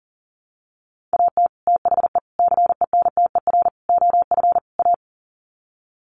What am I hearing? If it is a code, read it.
Morse code, heard as a keyed tone, about 40 wpm, 700 Hz. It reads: AT THE CENTER OF A
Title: Sets of 5 Words Encore2 40wpm